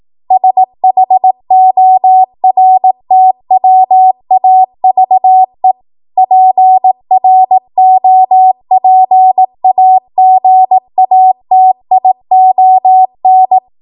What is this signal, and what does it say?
Morse code, heard as a keyed tone, about 18 wpm, 750 Hz. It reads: SHORTWAVE PROPAGATION
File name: Morse / 18WPM